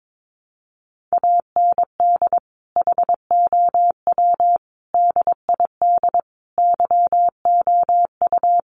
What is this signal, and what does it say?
Morse code, heard as a keyed tone, about 22 wpm, 700 Hz. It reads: AND HOW DID YOU
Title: Sentences from Top 1400 Words 22wpm